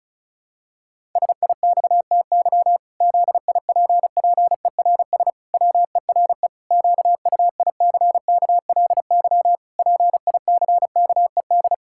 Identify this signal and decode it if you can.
Morse code, heard as a keyed tone, about 36 wpm, 700 Hz. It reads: SIXTY ZIPPERS WERE QUICKLY PICKED
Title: Pangram Challenge v2 - 35wpm